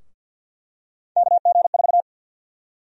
Morse code, heard as a keyed tone, about 48 wpm, 700 Hz. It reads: KG4
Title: USA Call Sign Prefixes - 50wpm